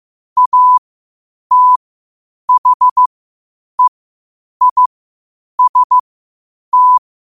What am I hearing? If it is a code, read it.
Morse code, heard as a keyed tone, about 15 wpm, spaced as for 8 wpm, 1000 Hz. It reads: ATHEIST
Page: Quotes for Fri, 15 Aug 2025 in Morse Code at 8 words per minute.